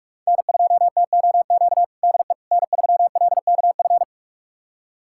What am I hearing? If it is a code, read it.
Morse code, heard as a keyed tone, about 42 wpm, 700 Hz. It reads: N2TOQ DE N3LKF
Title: USA Callsign DE Callsign - 5 Characters - 45wpm